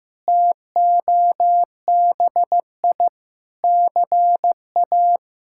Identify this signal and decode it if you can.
Morse code, heard as a keyed tone, about 15 wpm, 700 Hz. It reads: TOBI CA
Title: K1USN SST QSO Practice - 15wpm